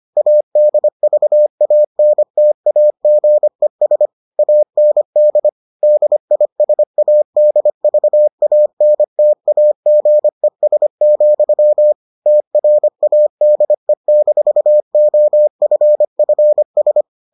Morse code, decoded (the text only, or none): ADVANTAGES AND DISADVANTAGES, TRADE-OFFS